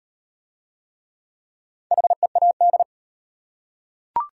Morse code, heard as a keyed tone, about 38 wpm, 700 Hz. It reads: HEAD